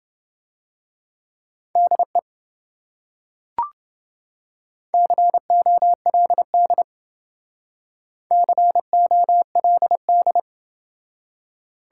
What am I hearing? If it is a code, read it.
Morse code, heard as a keyed tone, about 28 wpm, 700 Hz. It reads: DE COLD COLD